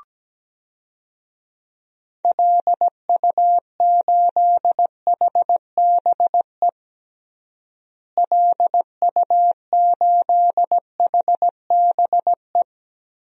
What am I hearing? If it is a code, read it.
Morse code, heard as a keyed tone, about 17 wpm, 700 Hz. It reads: LU8HBE LU8HBE
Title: DXCC-100 Call Signs - 6 Characters - Repeated 3x - 17wpm